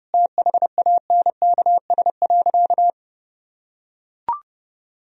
Morse code, heard as a keyed tone, about 30 wpm, 700 Hz. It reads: THANKS.
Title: Sentences from Top 1000 Words - Review Part 3 - 30wpm